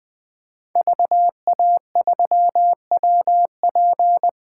Morse code, heard as a keyed tone, about 20 wpm, 700 Hz. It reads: VA3WP
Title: 2023 POTA Activator Call Signs - 20wpm